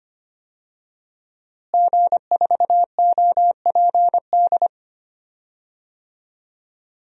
Morse code, heard as a keyed tone, about 25 wpm, 700 Hz. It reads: G4OPD
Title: DXCC-100 Call Signs - 5 Characters - Repeated 3x - 25wpm